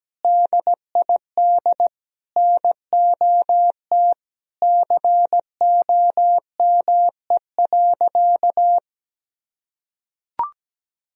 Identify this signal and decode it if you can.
Morse code, heard as a keyed tone, about 17 wpm, 700 Hz. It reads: DID NOT COME.
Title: Sentences from Top 1000 Words - Review Part 4 - 17wpm